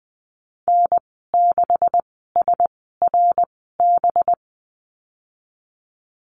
Morse code, heard as a keyed tone, about 20 wpm, spaced as for 15 wpm, 700 Hz. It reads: N6SRB
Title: 193 Lesson 31g - Letter v - Call Signs 20wpm